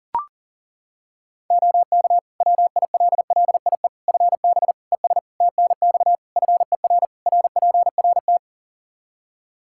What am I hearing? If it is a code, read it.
Morse code, heard as a keyed tone, about 40 wpm, 700 Hz. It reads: OK WILLIE FB ES TNX FER RPRT
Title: Practice QSO - Exchange 2 - Info - 40wpm